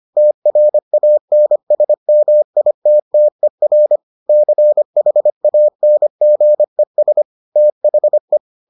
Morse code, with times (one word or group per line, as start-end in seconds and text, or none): 0.2-4.0 TRANSMITTER
4.3-7.2 CHANGES
7.6-8.4 THE